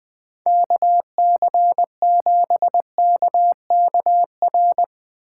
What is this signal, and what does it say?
Morse code, heard as a keyed tone, about 22 wpm, 700 Hz. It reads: KC7KKR
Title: Speed Racing - USA 2x3 Call Signs 20wpm